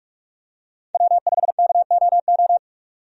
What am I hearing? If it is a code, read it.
Morse code, heard as a keyed tone, about 42 wpm, 700 Hz. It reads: W5KOO
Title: USA Call Signs - 5 Characters - 45wpm